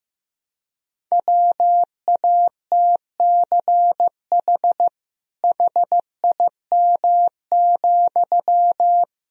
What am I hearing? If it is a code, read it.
Morse code, heard as a keyed tone, about 15 wpm, 700 Hz. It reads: WATCH HIM,
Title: Sentences from Top 300 Words and Abbreviations 15wpm